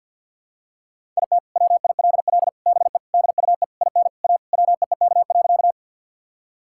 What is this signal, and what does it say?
Morse code, heard as a keyed tone, about 50 wpm, 700 Hz. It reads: IT WILL BE DUE IN A WEEK.
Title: Sentences from Top 1000 Words - Review Part 6 - 50wpm